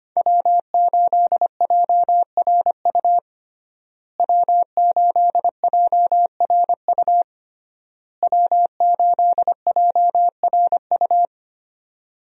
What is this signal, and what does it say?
Morse code, heard as a keyed tone, about 25 wpm, 700 Hz. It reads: W8JRU W8JRU W8JRU
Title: USA Call Signs - 5 Characters - Repeated 3x - 25wpm